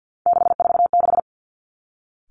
Morse code, heard as a keyed tone, about 50 wpm, 700 Hz. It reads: 646
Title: Three Challenging Letter-Number Combinations 50wpm